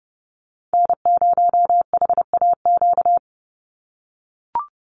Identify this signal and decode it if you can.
Morse code, heard as a keyed tone, about 30 wpm, 700 Hz. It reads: N0HAQ
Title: USA Call Signs 30wpm